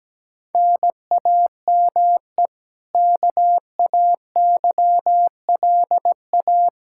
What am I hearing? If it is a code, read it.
Morse code, heard as a keyed tone, about 17 wpm, 700 Hz. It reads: NAME KAYLA
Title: Practice QSO - Exchange 1 - Beginning - 17wpm